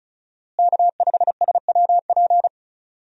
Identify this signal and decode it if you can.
Morse code, heard as a keyed tone, about 34 wpm, 700 Hz. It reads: K5SWP
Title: USA Call Signs - 5 Characters - Repeated 2x - 35wpm